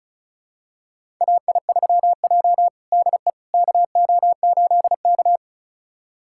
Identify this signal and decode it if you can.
Morse code, heard as a keyed tone, about 36 wpm, 700 Hz. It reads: AI3J DE KO8K
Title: USA Callsign DE Callsign - 4 Characters - 35wpm